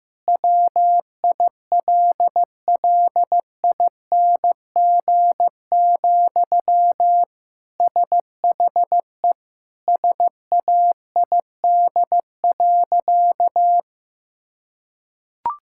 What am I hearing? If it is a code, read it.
Morse code, heard as a keyed tone, about 15 wpm, 700 Hz. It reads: WILLING, SHE SAID.
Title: Sentences from Top 1600 Words 15wpm